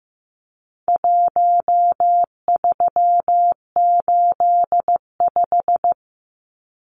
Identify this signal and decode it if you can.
Morse code, heard as a keyed tone, about 15 wpm, 700 Hz. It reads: 1385
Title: Numbers - 4-Digit 15wpm